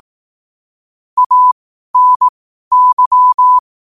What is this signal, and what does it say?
Morse code, heard as a keyed tone, about 18 wpm, spaced as for 12 wpm, 1000 Hz. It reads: ANY
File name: qotdfiles / 12wpm